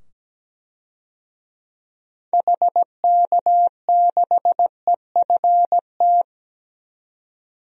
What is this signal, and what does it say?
Morse code, heard as a keyed tone, about 17 wpm, 700 Hz. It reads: HK6EFT
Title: DXCC-100 Call Signs - 6 Characters - 17wpm